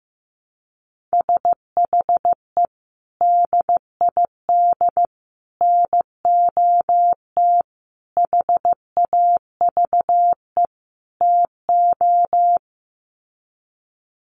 Morse code, heard as a keyed tone, about 15 wpm, 700 Hz. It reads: SHE DID NOT HAVE TO
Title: Sets of 5 Words Encore2 15wpm